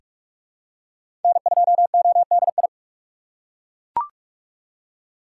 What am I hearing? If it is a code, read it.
Morse code, heard as a keyed tone, about 41 wpm, 700 Hz. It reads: N2ODI
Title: USA Call Signs - 5 Characters - 2x Character Spacing - 45wpm